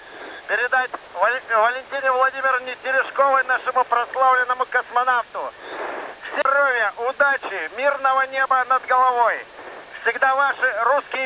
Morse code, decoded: none

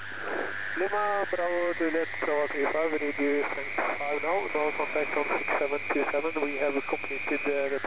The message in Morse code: none

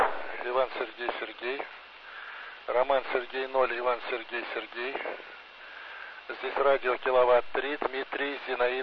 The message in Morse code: none

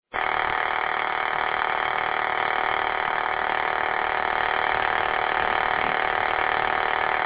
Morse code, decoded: none

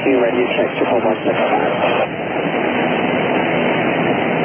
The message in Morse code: none

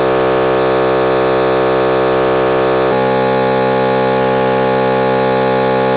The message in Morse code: none

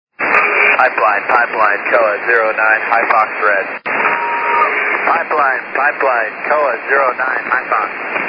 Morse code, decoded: none

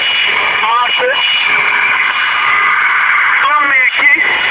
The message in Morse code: none

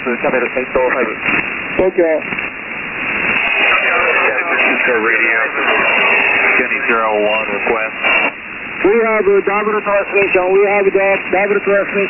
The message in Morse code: none

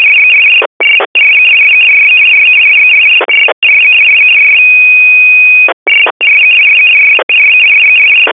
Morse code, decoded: none

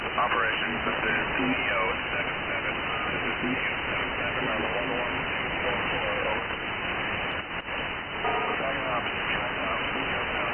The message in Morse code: none

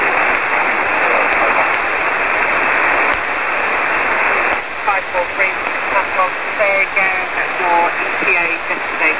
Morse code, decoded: none

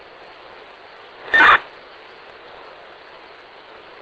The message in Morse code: none

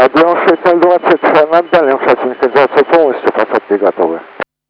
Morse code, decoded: none

